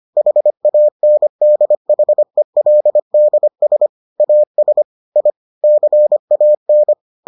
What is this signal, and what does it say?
Morse code, heard as a keyed tone, about 25 wpm, 600 Hz. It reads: HANDHELDS AS I CAN